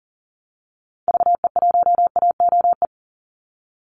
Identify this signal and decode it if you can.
Morse code, heard as a keyed tone, about 40 wpm, 700 Hz. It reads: VE1AOE
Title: Call Signs Repeated 3x - 40wpm